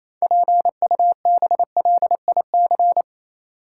Morse code, heard as a keyed tone, about 28 wpm, 700 Hz. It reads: PUBLIC
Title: Sentences from Top 600 Words - New Word Drill - 28wpm